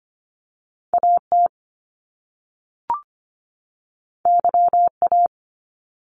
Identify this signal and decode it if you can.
Morse code, heard as a keyed tone, about 25 wpm, 700 Hz. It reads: AT YA
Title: Top Two Letter Combinations in Words 25wpm